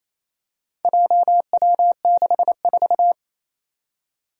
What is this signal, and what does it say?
Morse code, heard as a keyed tone, about 28 wpm, 700 Hz. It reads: JW64
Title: Four Challenging Letter-Number Combinations 28wpm